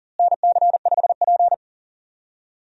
Morse code, heard as a keyed tone, about 40 wpm, 700 Hz. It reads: NC5P
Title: USA Call Signs - 4 Characters - Repeated 2x - 40wpm